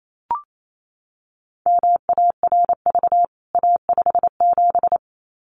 Morse code, heard as a keyed tone, about 28 wpm, 700 Hz. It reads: MARV A57
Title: CWT Exchange 28wpm